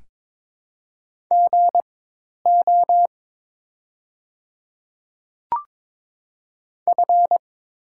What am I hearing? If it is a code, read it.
Morse code, heard as a keyed tone, about 22 wpm, spaced as for 10 wpm, 700 Hz. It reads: GO F
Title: 4x Character Spacing - Top 300 Words - 22wpm